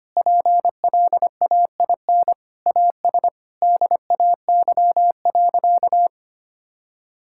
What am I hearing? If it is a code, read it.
Morse code, heard as a keyed tone, about 25 wpm, 700 Hz. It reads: PLAIN AS DAY.
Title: Sentences from Top 1000 Words - Review Part 5 - 25wpm